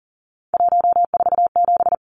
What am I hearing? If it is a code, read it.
Morse code, heard as a keyed tone, about 40 wpm, 700 Hz. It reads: 147
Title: Numbers - 3-Digit 40wpm